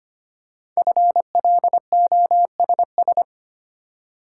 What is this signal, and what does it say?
Morse code, heard as a keyed tone, about 25 wpm, 700 Hz. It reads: FLOSS
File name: practice / 25wpm / Easy Words with Hints - 25wpm